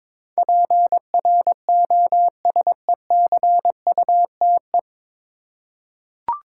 Morse code, heard as a keyed tone, about 21 wpm, 700 Hz. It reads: PROSECUTE
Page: Morse Code Ninja practice 22wpm